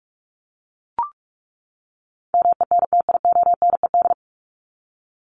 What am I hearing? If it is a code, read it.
Morse code, heard as a keyed tone, about 44 wpm, 700 Hz. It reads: MENTIONED
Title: ED Suffix - Top 100 Words with Base Words 45wpm